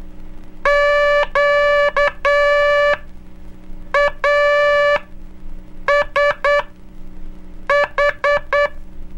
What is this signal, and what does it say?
Morse code, heard as a keyed tone, about 9 wpm, spaced as for 6 wpm, 580 Hz. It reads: QASH
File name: cursos / Curso CW mp3 / CW Aula - 04